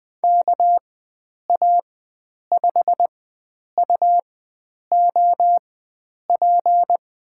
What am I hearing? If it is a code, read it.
Morse code, heard as a keyed tone, about 20 wpm, spaced as for 9 wpm, 700 Hz. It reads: KA5UOP
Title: USA Call Signs - 6 Characters - 4x Character Spacing - 20wpm